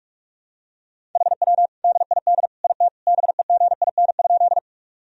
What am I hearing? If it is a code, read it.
Morse code, heard as a keyed tone, about 41 wpm, 700 Hz. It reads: HW DID IT BEGIN?
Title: Sentences from Top 300 Words and Abbreviations 45wpm